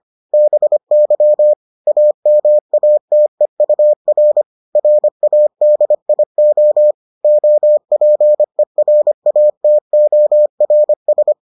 Morse code, heard as a keyed tone, about 25 wpm, 600 Hz. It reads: BY AMATEUR RADIO OPERATORS